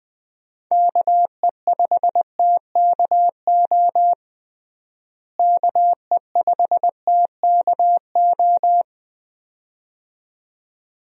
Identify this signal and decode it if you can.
Morse code, heard as a keyed tone, about 20 wpm, 700 Hz. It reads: KE5TKO KE5TKO
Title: USA Call Signs - 6 Characters - Repeated 2x - 20wpm